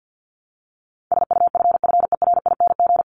Morse code, heard as a keyed tone, about 50 wpm, 700 Hz. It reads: SUFFERING
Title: ING Suffix - Top 100 Words with Base Words 50wpm